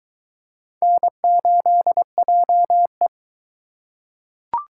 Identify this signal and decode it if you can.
Morse code, heard as a keyed tone, about 20 wpm, 700 Hz. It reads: N8JE